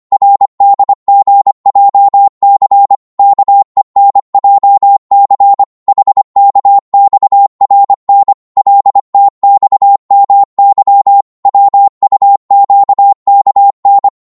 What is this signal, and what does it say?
Morse code, heard as a keyed tone, about 25 wpm, 800 Hz. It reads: RDGJC KENJC HKXRN LTXMY WUQKN
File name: Morse / ltr-25wpm